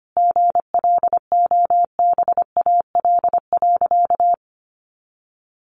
Morse code, heard as a keyed tone, about 25 wpm, 700 Hz. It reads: GLOBAL.
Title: Sentences from Top 1200 Words 25wpm